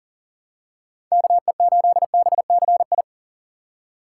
Morse code, heard as a keyed tone, about 38 wpm, 700 Hz. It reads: KE8BCI
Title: USA Call Signs - 6 Characters - 3x Character Spacing - 40wpm